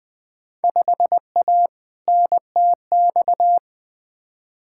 Morse code, heard as a keyed tone, about 20 wpm, 700 Hz. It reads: 5A NTX
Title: Field Day Exchanges - 20wpm